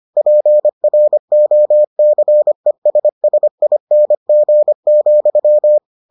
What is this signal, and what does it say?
Morse code, heard as a keyed tone, about 25 wpm, 600 Hz. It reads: PROCESSING,